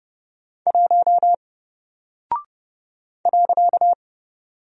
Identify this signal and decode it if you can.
Morse code, heard as a keyed tone, about 30 wpm, 700 Hz. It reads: 1 .
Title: Single Character (Letter:Number:Punctuation:Prosign) 30wpm